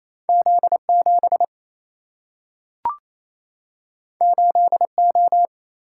Morse code, heard as a keyed tone, about 28 wpm, 700 Hz. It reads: Z 7 8 O
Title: Single-Letter Single-Number 28wpm